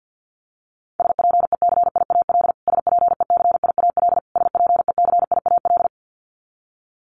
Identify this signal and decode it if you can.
Morse code, heard as a keyed tone, about 50 wpm, 700 Hz. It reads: SPECIAL SPECIAL SPECIAL
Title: Top 500 Words Repeated 50wpm